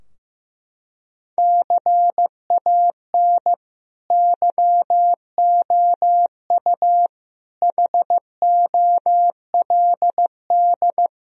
Morse code, heard as a keyed tone, about 15 wpm, 700 Hz. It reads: CAN YOU HOLD
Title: Sentences from Top 1000 Words - Review Part 3 - 15wpm